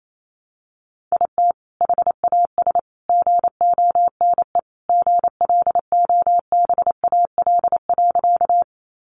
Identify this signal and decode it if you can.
Morse code, heard as a keyed tone, about 28 wpm, 700 Hz. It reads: IT HAS GONE GLOBAL.
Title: Sentences from Top 1200 Words 28wpm